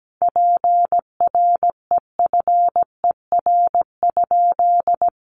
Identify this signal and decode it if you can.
Morse code, heard as a keyed tone, about 17 wpm, 700 Hz. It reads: PREFER?
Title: Sentences from Top 1100 Words 17wpm